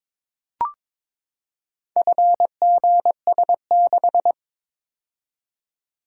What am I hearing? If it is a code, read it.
Morse code, heard as a keyed tone, about 22 wpm, 700 Hz. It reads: FGS6
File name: practice / 22wpm / Four Letter-Number Combinations - 22wpm